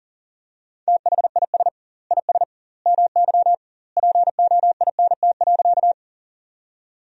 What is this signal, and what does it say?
Morse code, heard as a keyed tone, about 40 wpm, 700 Hz. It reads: THIS IS MY POINT.
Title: 2x Word-Spacing - Sentences from Top 200 Words - 40wpm